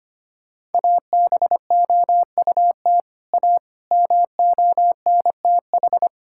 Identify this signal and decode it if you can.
Morse code, heard as a keyed tone, about 25 wpm, 700 Hz. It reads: ABOUT A MONTH
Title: Sentences from Top 1000 Words - Review Part 5 - 25wpm